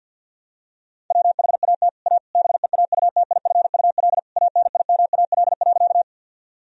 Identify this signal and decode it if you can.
Morse code, heard as a keyed tone, about 49 wpm, 700 Hz. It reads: WHAT A BEAUTIFUL ANIMAL.
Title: Sentences from Top 1000 Words - Review Part 1 - 50wpm